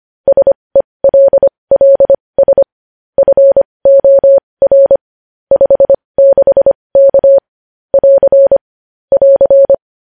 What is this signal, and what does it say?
Morse code, heard as a keyed tone, about 25 wpm, spaced as for 20 wpm, 550 Hz. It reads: SELLS FOR 56K <AR> <AR>
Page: news-headlines-in-morse-code-at-20-wpm